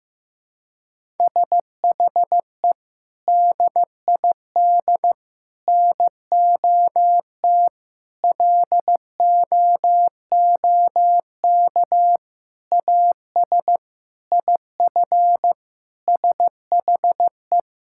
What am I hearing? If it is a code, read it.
Morse code, heard as a keyed tone, about 15 wpm, 700 Hz. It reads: SHE DID NOT LOOK AS IF SHE
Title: Sentences from Top 1000 Words - Review Part 6 - 15wpm